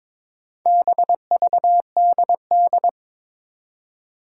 Morse code, heard as a keyed tone, about 22 wpm, 700 Hz. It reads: BVDD
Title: Four Challenging Letter-Number Combinations 22wpm